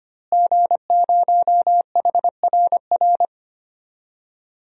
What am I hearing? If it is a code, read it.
Morse code, heard as a keyed tone, about 25 wpm, 700 Hz. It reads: G0HRR
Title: DXCC-100 Call Signs - 5 Characters - Repeated 3x - 25wpm